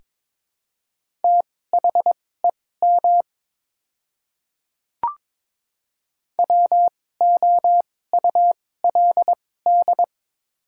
Morse code, heard as a keyed tone, about 23 wpm, spaced as for 16 wpm, 700 Hz. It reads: THEM WOULD
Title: 2x Character Spacing - Top 200 Words - 22wpm